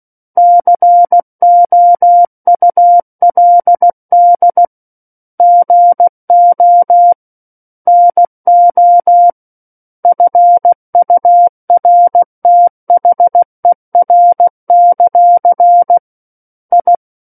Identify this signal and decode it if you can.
Morse code, heard as a keyed tone, about 16 wpm, 700 Hz. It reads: COULD GO NO FURTHER; I
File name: Book One / 07-Chapter 7 - 16 WPM